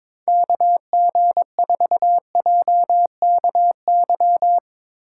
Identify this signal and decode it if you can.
Morse code, heard as a keyed tone, about 24 wpm, 700 Hz. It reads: KG4JKY
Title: Speed Racing - USA 2x3 Call Signs 22wpm